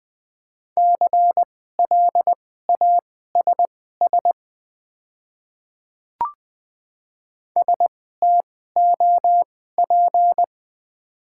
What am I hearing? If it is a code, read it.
Morse code, heard as a keyed tone, about 20 wpm, spaced as for 14 wpm, 690 Hz. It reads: CLASS STOP
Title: 2x Character Spacing - Top 500 Words - 20wpm